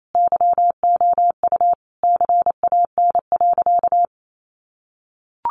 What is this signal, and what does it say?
Morse code, heard as a keyed tone, about 28 wpm, 700 Hz. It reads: YOU CAN.
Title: Sentences from Top 500 Words 28wpm